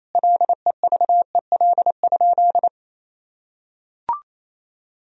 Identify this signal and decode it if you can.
Morse code, heard as a keyed tone, about 28 wpm, 700 Hz. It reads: LEVEL?
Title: Sentences from Top 1000 Words - Review Part 4 - 28wpm